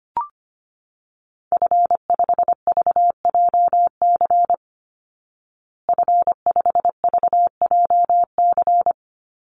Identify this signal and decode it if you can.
Morse code, heard as a keyed tone, about 25 wpm, 700 Hz. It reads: F5VJC F5VJC
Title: Call Signs Repeated 3x - 25wpm